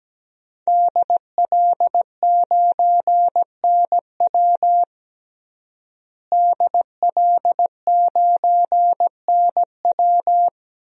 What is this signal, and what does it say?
Morse code, heard as a keyed tone, about 17 wpm, 700 Hz. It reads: DL9NW DL9NW
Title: DXCC-100 Call Signs - 5 Characters - Repeated 2x - 17wpm